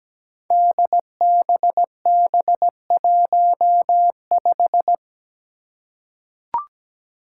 Morse code, heard as a keyed tone, about 17 wpm, 700 Hz. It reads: DBB15
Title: Five Challenging Letter-Number Combinations 17wpm